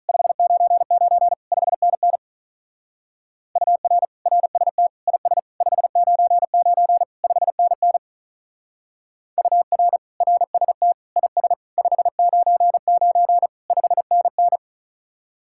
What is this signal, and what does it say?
Morse code, heard as a keyed tone, about 45 wpm, 700 Hz. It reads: 599 5NN UR RST IS 599 5NN UR RST IS 599 5NN